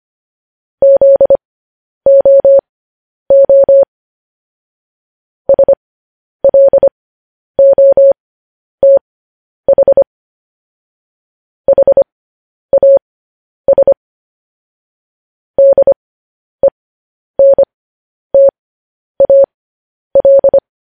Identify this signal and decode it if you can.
Morse code, heard as a keyed tone, about 25 wpm, spaced as for 10 wpm, 550 Hz. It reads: ZOO SLOTH HAS DENTAL
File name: podcast / News headlines-1x-25x10